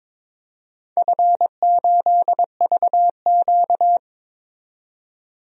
Morse code, heard as a keyed tone, about 21 wpm, 700 Hz. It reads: F8VQ
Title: DXCC-100 Call Signs - 4 Characters - 22wpm